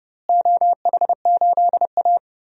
Morse code, heard as a keyed tone, about 30 wpm, 700 Hz. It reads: OH8A
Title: DXCC-100 Call Signs - 4 Characters - Repeated 3x - 30wpm